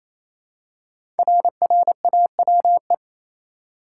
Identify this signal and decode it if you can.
Morse code, heard as a keyed tone, about 27 wpm, 700 Hz. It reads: RRAWE
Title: Code Groups - 5 Letters - 28wpm